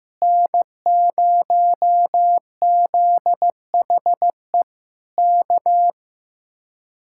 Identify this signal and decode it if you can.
Morse code, heard as a keyed tone, about 15 wpm, 700 Hz. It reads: N0ZHE K
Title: Speed-Racing Common QSO Phrases Encore - 15wpm